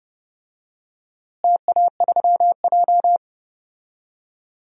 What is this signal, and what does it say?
Morse code, heard as a keyed tone, about 29 wpm, 700 Hz. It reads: TA3J
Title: DXCC-100 Call Signs - 4 Characters - Repeated 3x - 30wpm